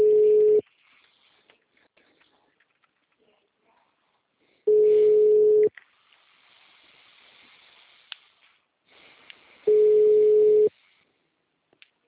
Ringback tone (Europe, 425 Hz).